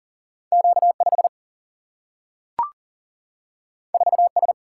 Morse code, heard as a keyed tone, about 40 wpm, 700 Hz. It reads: Q 5 4 S
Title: Single-Letter Single-Number 40wpm